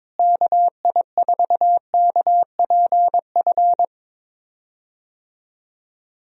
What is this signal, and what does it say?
Morse code, heard as a keyed tone, about 22 wpm, 700 Hz. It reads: KI4KPF
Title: USA Call Signs - 6 Characters - Repeated 3x - 22wpm